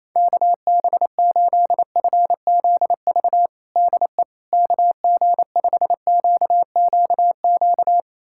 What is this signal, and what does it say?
Morse code, heard as a keyed tone, about 27 wpm, 700 Hz. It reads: KB8FZV DE KG5QQQ
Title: USA Callsign DE Callsign - 6 Characters - 28wpm